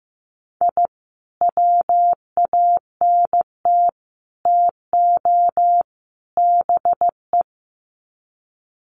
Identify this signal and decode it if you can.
Morse code, heard as a keyed tone, about 15 wpm, 700 Hz. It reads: I WANT TO BE
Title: Sets of 4 Words 15wpm